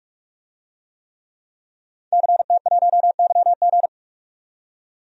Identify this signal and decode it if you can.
Morse code, heard as a keyed tone, about 42 wpm, 700 Hz. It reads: CT1YG
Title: DXCC-100 Call Signs - 5 Characters - 45wpm